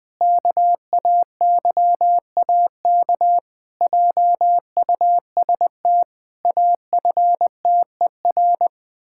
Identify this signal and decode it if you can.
Morse code, heard as a keyed tone, about 20 wpm, 700 Hz. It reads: KAYAK JUST AFTER
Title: Pangram Challenge v2 - 20wpm